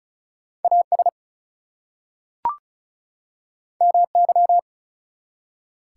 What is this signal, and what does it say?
Morse code, heard as a keyed tone, about 36 wpm, 700 Hz. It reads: AS MY
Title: Two Letter Words - 35wpm